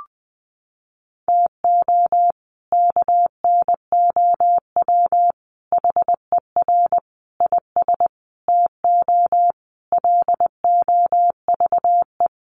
Morse code, heard as a keyed tone, about 20 wpm, 700 Hz. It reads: TO KNOW HER IS TO LOVE
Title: Sentences from Top 400 Words 20wpm